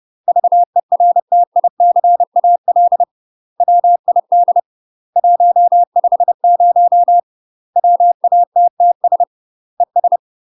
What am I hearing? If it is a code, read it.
Morse code, heard as a keyed tone, about 30 wpm, 700 Hz. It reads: VERTICAL WID 150 WATTS ES
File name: UK MASTER QSOs TRAINING / 30WPM / UK MASTER QSOs 0022